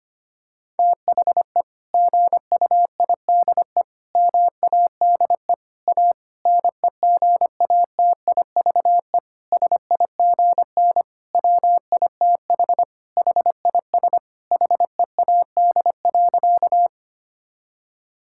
Morse code, heard as a keyed tone, about 25 wpm, 700 Hz. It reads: THE GUIDE MADE A NEGATIVE SIGN WITH HIS HEAD.
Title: Sentences from Top 1000 Words - Review Part 6 - 25wpm